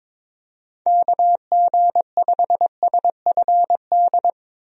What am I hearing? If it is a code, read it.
Morse code, heard as a keyed tone, about 22 wpm, 700 Hz. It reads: KG5SFD
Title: USA Call Signs - 6 Characters - 22wpm